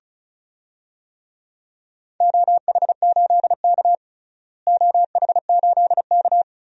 Morse code, heard as a keyed tone, about 36 wpm, 690 Hz. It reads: OH8K OH8K
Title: DXCC-100 Call Signs - 4 Characters - Repeated 3x - 35wpm